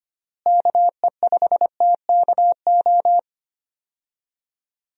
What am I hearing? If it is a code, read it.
Morse code, heard as a keyed tone, about 25 wpm, 700 Hz. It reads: KE5TKO
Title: USA Call Signs - 6 Characters - Repeated 2x - 25wpm